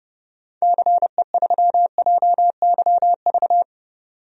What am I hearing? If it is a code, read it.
Morse code, heard as a keyed tone, about 30 wpm, 700 Hz. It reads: CE3JYV
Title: DXCC-100 Call Signs - 6 Characters - 30wpm